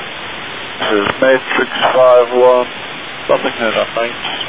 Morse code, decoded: none

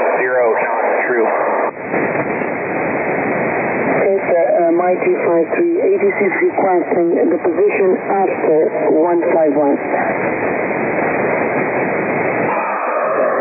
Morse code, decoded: none

